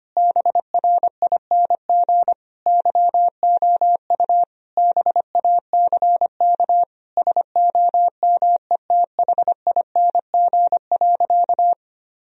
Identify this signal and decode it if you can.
Morse code, heard as a keyed tone, about 25 wpm, 700 Hz. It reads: BRING YOU BACK SOMETHING.
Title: Sentences from Top 1000 Words - Review Part 3 - 25wpm